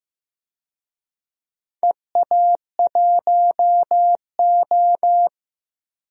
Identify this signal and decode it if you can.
Morse code, heard as a keyed tone, about 15 wpm, 700 Hz. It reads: EA1O
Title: DXCC-100 Call Signs - 4 Characters - Repeated 2x - 15wpm